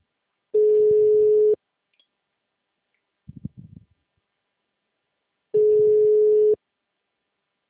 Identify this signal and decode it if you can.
Ringback tone (Europe, 425 Hz), with speech.